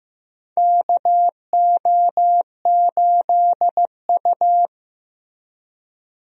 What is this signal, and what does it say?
Morse code, heard as a keyed tone, about 15 wpm, 700 Hz. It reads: KO8U
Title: USA Call Signs - 4 Characters - Repeated 3x - 15wpm